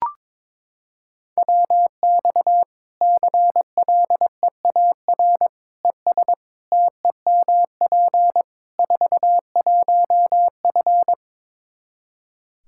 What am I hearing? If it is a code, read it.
Morse code, heard as a keyed tone, about 22 wpm, 700 Hz. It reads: WX CLEAR ES TEMP 41F
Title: K1USN SST QSO Practice 22wpm